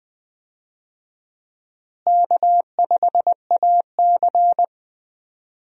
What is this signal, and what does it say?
Morse code, heard as a keyed tone, about 20 wpm, 700 Hz. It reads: K5AC
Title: USA Call Signs - 4 Characters - Repeated 2x - 20wpm